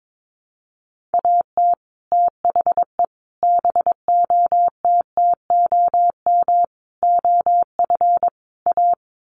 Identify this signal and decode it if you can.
Morse code, heard as a keyed tone, about 22 wpm, 700 Hz. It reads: AT THE BOTTOM OF A
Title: Sets of 5 Words Encore2 22wpm